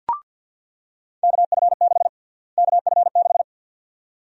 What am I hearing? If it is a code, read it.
Morse code, heard as a keyed tone, about 50 wpm, 700 Hz. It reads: KF6 KF6
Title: USA Call Sign Prefixes - Repeated 2x - 50wpm